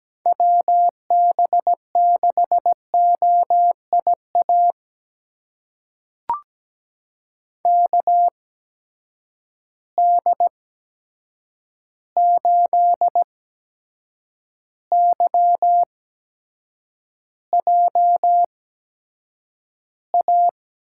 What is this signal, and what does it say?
Morse code, heard as a keyed tone, about 17 wpm, spaced as for 7 wpm, 700 Hz. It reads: WB6OIA KD8YJA
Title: USA Call Signs - 8x Character Spacing - 17wpm